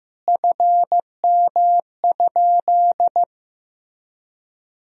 Morse code, heard as a keyed tone, about 16 wpm, 700 Hz. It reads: FM?
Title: Sentences from Top 400 Words and Abbreviations 15wpm